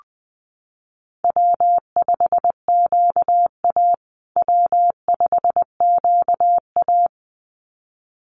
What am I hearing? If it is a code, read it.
Morse code, heard as a keyed tone, about 20 wpm, 700 Hz. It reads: W5QA W5QA
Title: USA Call Signs Repeated 2x 20wpm